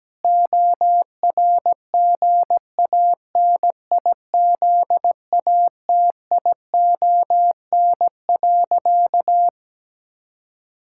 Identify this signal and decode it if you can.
Morse code, heard as a keyed tone, about 17 wpm, 700 Hz. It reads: ORGANIZATION.
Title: Sentences from Top 1000 Words - Review Part 6 - 17wpm